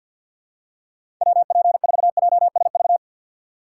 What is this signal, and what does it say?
Morse code, heard as a keyed tone, about 51 wpm, 700 Hz. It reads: WP4JSV
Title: USA Call Signs - 6 Characters - 3x Character Spacing - 50wpm